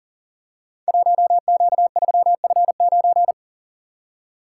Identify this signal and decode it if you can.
Morse code, heard as a keyed tone, about 39 wpm, 700 Hz. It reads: 1Q3F9
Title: Code Groups - 5 Letter:Numbers - 40wpm